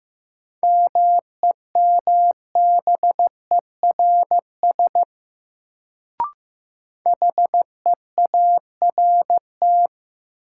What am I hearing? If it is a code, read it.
Morse code, heard as a keyed tone, about 15 wpm, 700 Hz. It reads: MEMBERS HEART
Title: Sets of 3 Words - New Word Drill - 15wpm